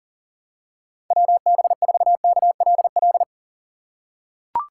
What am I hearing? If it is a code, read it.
Morse code, heard as a keyed tone, about 40 wpm, 700 Hz. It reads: WB4KLL
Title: USA Call Signs - 6 Characters - 2x Character Spacing - 40wpm